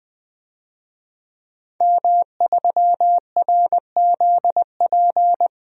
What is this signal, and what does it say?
Morse code, heard as a keyed tone, about 21 wpm, 700 Hz. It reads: M3RZP
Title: DXCC-100 Call Signs - 5 Characters - Repeated 2x - 20wpm